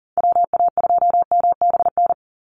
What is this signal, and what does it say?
Morse code, heard as a keyed tone, about 40 wpm, 700 Hz. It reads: WA2MBN
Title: USA Call Signs 40wpm